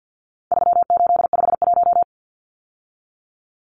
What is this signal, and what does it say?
Morse code, heard as a keyed tone, about 50 wpm, 700 Hz. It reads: 3851
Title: Numbers - 4-Digit 50wpm